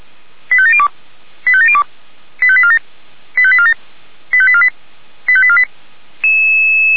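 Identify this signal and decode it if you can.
Five-tone selective call (ZVEI), heard as quick runs of five tones, tones of about 70 ms. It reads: 85702 85702 85847 85847 85848 85848
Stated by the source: ZVEI five-tone call received by SDR.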